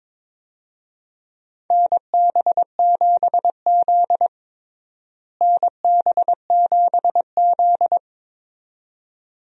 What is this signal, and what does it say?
Morse code, heard as a keyed tone, about 22 wpm, 700 Hz. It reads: NB7Z NB7Z
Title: USA Call Signs - 4 Characters - Repeated 2x - 22wpm